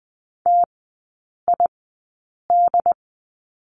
Morse code, heard as a keyed tone, about 20 wpm, spaced as for 8 wpm, 700 Hz. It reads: TID
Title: 093 Lesson 16e - Letter c - Three Characters 20wpm